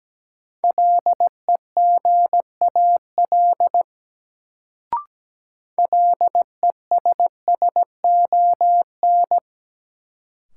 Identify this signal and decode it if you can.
Morse code, heard as a keyed tone, about 17 wpm, 700 Hz. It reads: LEGAL LESSON
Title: Sentences from Top 1200 Words - New Word Drill - 17wpm